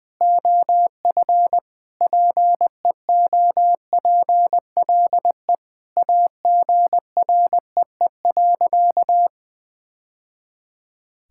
Morse code, heard as a keyed tone, about 20 wpm, 700 Hz. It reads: OF PEOPLE AGREE.
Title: Sentences with 3-Word Phrases - 20wpm